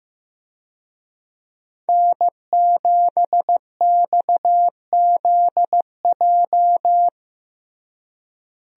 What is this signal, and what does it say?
Morse code, heard as a keyed tone, about 15 wpm, 700 Hz. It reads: N7XZJ
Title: USA Call Signs - 5 Characters - Repeated 3x - 15wpm